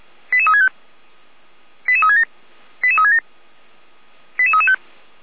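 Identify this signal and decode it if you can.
Five-tone selective call (ZVEI), heard as quick runs of five tones, tones of about 70 ms. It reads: 88365 88367 88367 88335
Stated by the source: ZVEI five-tone selective call received on SDR.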